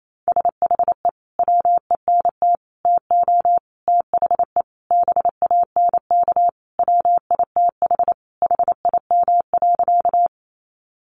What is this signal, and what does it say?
Morse code, heard as a keyed tone, about 28 wpm, 700 Hz. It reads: SHE WENT TO THE BANK WITH HIM.
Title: Sentences from Top 900 Words 28wpm